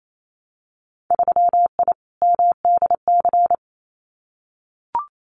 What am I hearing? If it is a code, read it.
Morse code, heard as a keyed tone, about 28 wpm, 700 Hz. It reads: 3I MDC
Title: Winter Field Day Exchanges Encore 28wpm